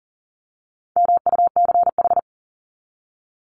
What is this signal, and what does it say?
Morse code, heard as a keyed tone, about 39 wpm, 700 Hz. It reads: MUCH
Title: Top 300 Words 40wpm